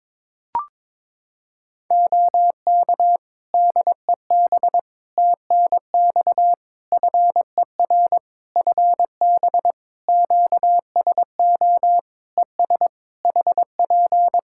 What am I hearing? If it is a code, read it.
Morse code, heard as a keyed tone, about 22 wpm, 700 Hz. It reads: OK DEB TNX FER FB QSO ES HP
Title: Practice QSO - Exchange 4 - Ending - 22wpm